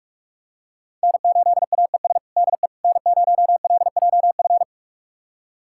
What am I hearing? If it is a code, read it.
Morse code, heard as a keyed tone, about 41 wpm, 700 Hz. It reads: N8AES DE N0LJF
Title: USA Callsign DE Callsign - 5 Characters - 45wpm